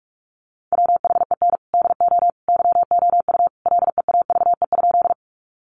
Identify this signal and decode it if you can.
Morse code, heard as a keyed tone, about 44 wpm, 700 Hz. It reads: WHEN DO YOU LEAVE?
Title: Sentences from Top 400 Words 45wpm